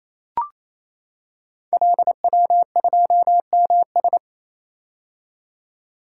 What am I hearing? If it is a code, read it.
Morse code, heard as a keyed tone, about 28 wpm, 700 Hz. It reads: LW2MS
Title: Code Groups - 5 Letter:Numbers - 28wpm